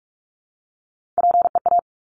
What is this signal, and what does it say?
Morse code, heard as a keyed tone, about 46 wpm, 700 Hz. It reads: PEA
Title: Top Three Letter Combinations in Words 45wpm